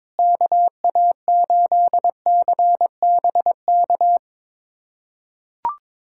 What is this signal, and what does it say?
Morse code, heard as a keyed tone, about 22 wpm, 700 Hz. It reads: KA8CBK
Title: USA Callsign DE Callsign - 6 Characters - 22wpm